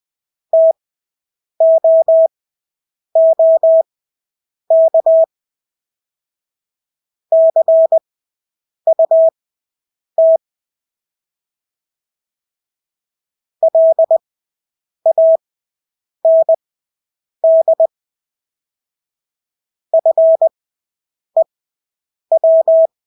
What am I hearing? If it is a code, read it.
Morse code, heard as a keyed tone, about 20 wpm, spaced as for 8 wpm, 650 Hz. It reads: TOOK CUT LAND FEW
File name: Engelske ord 20/8 wpm